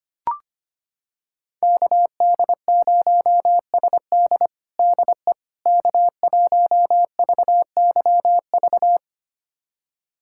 Morse code, heard as a keyed tone, about 25 wpm, 700 Hz. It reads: KD0SD DE K1VYV
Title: USA Callsign DE Callsign 25wpm